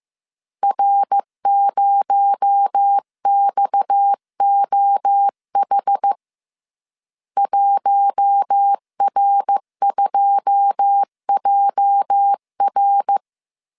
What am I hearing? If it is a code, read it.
Morse code, heard as a keyed tone, about 15 wpm, 800 Hz. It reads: R0XOH 1R2JR
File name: CW / DET / 1 9 1500C S C 14wpm 3245